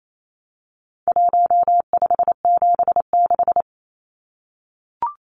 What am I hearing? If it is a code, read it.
Morse code, heard as a keyed tone, about 28 wpm, 700 Hz. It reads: 1576
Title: Numbers - Contest Style - 4-Digit 28wpm